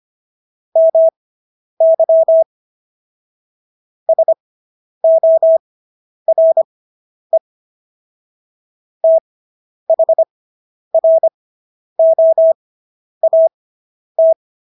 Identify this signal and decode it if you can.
Morse code, heard as a keyed tone, about 25 wpm, spaced as for 10 wpm, 650 Hz. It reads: MY SORE THROAT